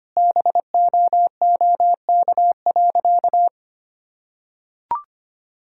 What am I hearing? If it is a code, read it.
Morse code, heard as a keyed tone, about 25 wpm, 700 Hz. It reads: BOOK.
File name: practice / 25wpm / Sentences with 3-Word Phrases - 25wpm